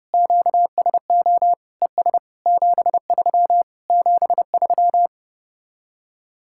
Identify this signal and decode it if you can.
Morse code, heard as a keyed tone, about 33 wpm, 700 Hz. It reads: QSO ES 73 73